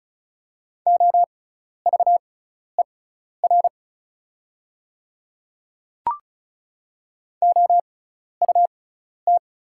Morse code, heard as a keyed tone, about 36 wpm, spaced as for 12 wpm, 690 Hz. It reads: OVER OUT
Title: 6x Character Spacing - Top 300 Words - 35wpm